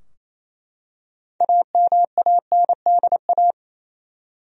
Morse code, heard as a keyed tone, about 28 wpm, 700 Hz. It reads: AMANDA
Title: USA Common Names - 28wpm